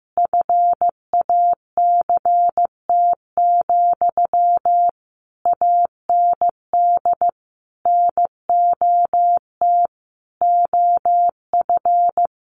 Morse code, heard as a keyed tone, about 14 wpm, 700 Hz. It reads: FACT, AND NOT OF
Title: Sentences from Top 1400 Words 15wpm